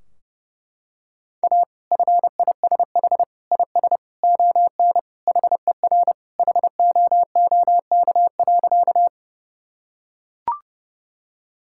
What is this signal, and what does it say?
Morse code, heard as a keyed tone, about 32 wpm, 700 Hz. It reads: A FISH IS ON HER HOOK.
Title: Sentences from Top 1600 Words 30wpm